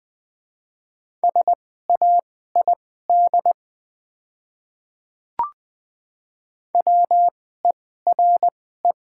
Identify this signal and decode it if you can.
Morse code, heard as a keyed tone, about 20 wpm, spaced as for 14 wpm, 700 Hz. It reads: SAID WERE
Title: 2x Character Spacing - Top 100 Words - 20wpm